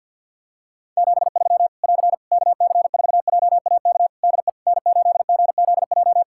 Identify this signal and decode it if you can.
Morse code, heard as a keyed tone, about 53 wpm, 700 Hz. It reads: 73 <AR> KC4JAK DE N8GZJ
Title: Practice QSO - Exchange 4 - Ending - 50wpm